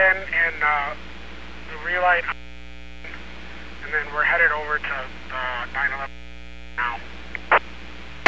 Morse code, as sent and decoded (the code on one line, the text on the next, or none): none